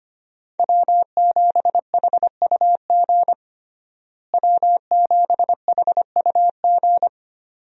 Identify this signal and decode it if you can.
Morse code, heard as a keyed tone, about 25 wpm, 700 Hz. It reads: W7HUG W7HUG
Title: USA Call Signs - 5 Characters - Repeated 2x - 25wpm